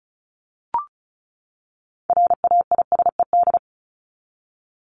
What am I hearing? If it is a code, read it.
Morse code, heard as a keyed tone, about 34 wpm, 700 Hz. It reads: RAISED
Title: ED Suffix - Top 100 Words with Base Words 35wpm